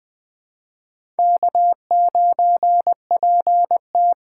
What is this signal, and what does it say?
Morse code, heard as a keyed tone, about 24 wpm, 700 Hz. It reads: K9PT
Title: Speed Racing - USA 1x2 Call Signs 20wpm